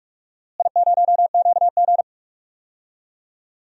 Morse code, heard as a keyed tone, about 46 wpm, 700 Hz. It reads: I0QG
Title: DXCC-100 Call Signs - 4 Characters - Repeated 2x - 45wpm